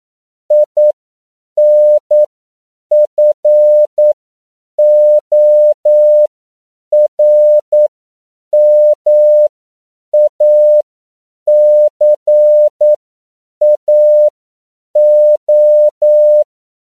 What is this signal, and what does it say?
Morse code, heard as a keyed tone, about 9 wpm, spaced as for 7 wpm, 600 Hz. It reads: INFORMACAO
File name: CW / 10PPM